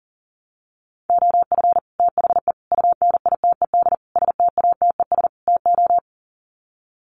Morse code, heard as a keyed tone, about 40 wpm, 700 Hz. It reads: OF THE UNITED STATES TO
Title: Sets of 5 Words Encore 40wpm